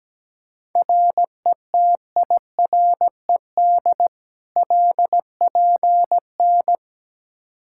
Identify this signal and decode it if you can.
Morse code, heard as a keyed tone, about 17 wpm, 700 Hz. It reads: RETIRED LPN
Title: Practice QSO - Exchange 3 - More Info - 17wpm